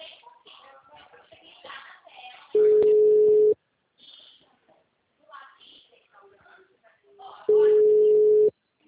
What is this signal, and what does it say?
Ringback tone (Europe, 425 Hz), with speech.